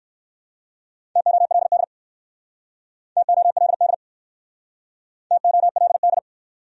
Morse code, heard as a keyed tone, about 61 wpm, 700 Hz. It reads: TOLD TOLD TOLD